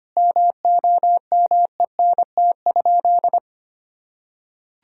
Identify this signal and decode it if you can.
Morse code, heard as a keyed tone, about 25 wpm, 700 Hz. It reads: MOMENT?
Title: Sentences from Top 1000 Words - Review Part 1 - 25wpm